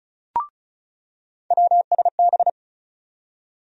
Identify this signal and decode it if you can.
Morse code, heard as a keyed tone, about 36 wpm, 700 Hz. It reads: WSB
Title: USA Call Sign Postfixes - 3 Letters 35wpm